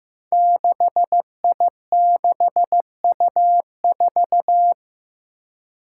Morse code, heard as a keyed tone, about 16 wpm, 700 Hz. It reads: 6I6U4
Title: Five Challenging Letter-Number Combinations 15wpm